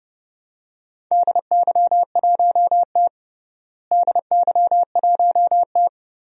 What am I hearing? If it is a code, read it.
Morse code, heard as a keyed tone, about 30 wpm, 700 Hz. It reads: DY1T DY1T
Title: DXCC-100 Call Signs - 4 Characters - Repeated 3x - 30wpm